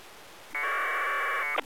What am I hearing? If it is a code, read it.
Packet radio (AX.25, as in APRS).